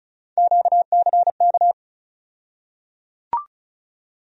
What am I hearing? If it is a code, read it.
Morse code, heard as a keyed tone, about 36 wpm, 700 Hz. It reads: QCK
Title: USA Call Sign Postfixes - 3 Letters 35wpm